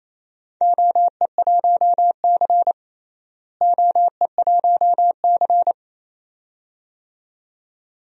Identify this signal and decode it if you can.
Morse code, heard as a keyed tone, about 28 wpm, 700 Hz. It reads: OE1C OE1C
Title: DXCC-100 Call Signs - 4 Characters - Repeated 2x - 28wpm